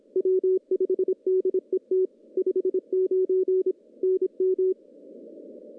Morse code, heard as a keyed tone, about 26 wpm, 370 Hz. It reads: W5DET 59 NM
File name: contests / CW-SS-k5tr- -1500